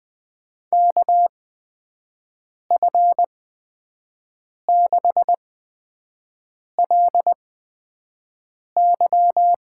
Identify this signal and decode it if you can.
Morse code, heard as a keyed tone, about 20 wpm, spaced as for 5 wpm, 700 Hz. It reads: KF6LY
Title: USA Call Signs - 8x Character Spacing - 20wpm